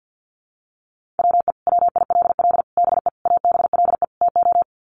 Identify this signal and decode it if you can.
Morse code, heard as a keyed tone, about 50 wpm, 700 Hz. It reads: WE WILL BE ABLE TO
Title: Sets of 5 Words Encore 50wpm